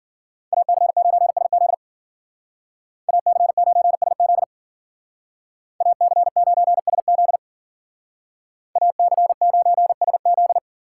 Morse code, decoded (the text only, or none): AC9SZ AC9SZ AC9SZ AC9SZ